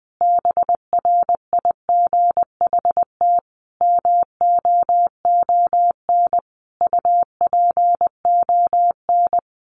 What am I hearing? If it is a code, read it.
Morse code, heard as a keyed tone, about 20 wpm, 700 Hz. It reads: BRIGHT MOON UPON
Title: Sentences from Top 1500 Words 20wpm